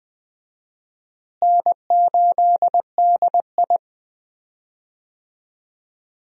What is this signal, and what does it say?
Morse code, heard as a keyed tone, about 20 wpm, 700 Hz. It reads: N8DI
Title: USA Call Signs - 4 Characters - Repeated 3x - 20wpm